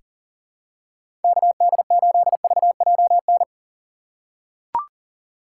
Morse code, heard as a keyed tone, about 40 wpm, 700 Hz. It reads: KD8VJN
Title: USA Call Signs - 6 Characters - 3x Character Spacing - 40wpm